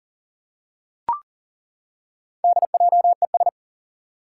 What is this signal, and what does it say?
Morse code, heard as a keyed tone, about 40 wpm, 700 Hz. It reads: DJES
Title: Four Challenging Letter-Number Combinations 40wpm